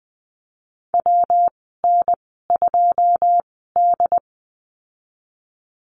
Morse code, heard as a keyed tone, about 20 wpm, spaced as for 15 wpm, 700 Hz. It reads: WN2D
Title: 122 Lesson 20g - Number 3 - Call Signs 20wpm